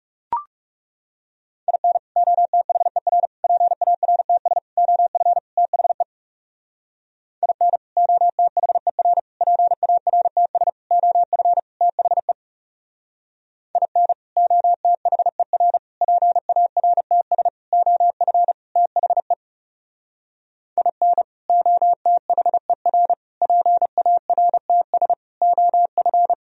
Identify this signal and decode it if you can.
Morse code, heard as a keyed tone, about 37 wpm, 700 Hz. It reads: IN OTHER PARTS OF THE IN OTHER PARTS OF THE IN OTHER PARTS OF THE IN OTHER PARTS OF
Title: Speed-Racing - Sets of 5 Words 30wpm